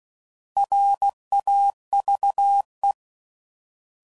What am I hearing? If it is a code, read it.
Morse code, heard as a keyed tone, about 16 wpm, 800 Hz. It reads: RAVE
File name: Prefixes / By First Letter / Rs